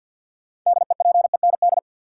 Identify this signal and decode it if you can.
Morse code, heard as a keyed tone, about 53 wpm, 700 Hz. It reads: DEPEND
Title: Sentences from Top 1300 Words - New Word Drill Alt - 50wpm